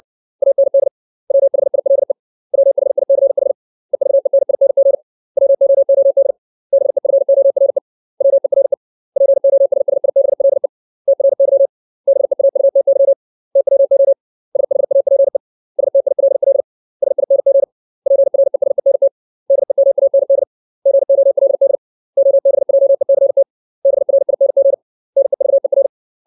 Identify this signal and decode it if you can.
Morse code, heard as a keyed tone, about 60 wpm, 550 Hz. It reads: AND WHILE WHICH EVENING WOOD BEFORE WERE POSSIBLE TRY BEAUTY TWO SHAPE STILL SING PRESENT DEMAND GOLD OBJECT BRING NEVER
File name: 632 words / 632 Words 60WPM